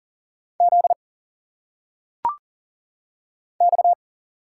Morse code, heard as a keyed tone, about 40 wpm, 700 Hz. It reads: Z X
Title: Single Letters - Phonetic Pronunciation 40wpm